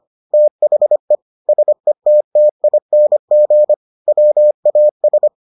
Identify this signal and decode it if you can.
Morse code, heard as a keyed tone, about 25 wpm, 600 Hz. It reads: THE SETTING WAS